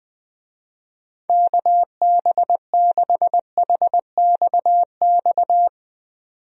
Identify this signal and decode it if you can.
Morse code, heard as a keyed tone, about 20 wpm, 700 Hz. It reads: KB6HXX
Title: USA Call Signs - 6 Characters - 20wpm